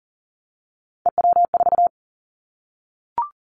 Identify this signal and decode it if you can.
Morse code, heard as a keyed tone, about 39 wpm, 700 Hz. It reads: EW4
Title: Callsign Prefixes for Top 100 DXCC - 40wpm